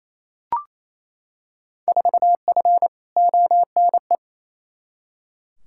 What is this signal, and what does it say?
Morse code, heard as a keyed tone, about 28 wpm, 700 Hz. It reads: 4F ONE
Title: Field Day Exchanges Encore - 28wpm